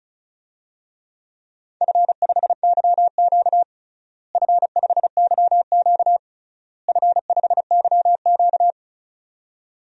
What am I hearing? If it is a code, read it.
Morse code, heard as a keyed tone, about 36 wpm, 700 Hz. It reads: F5YQ F5YQ F5YQ
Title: DXCC-100 Call Signs - 4 Characters - Repeated 3x - 35wpm